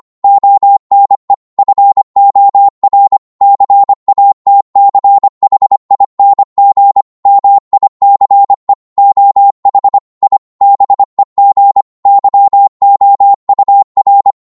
Morse code, heard as a keyed tone, about 25 wpm, 800 Hz. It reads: ONE FOR CATCHING MICE OH I BEG YOUR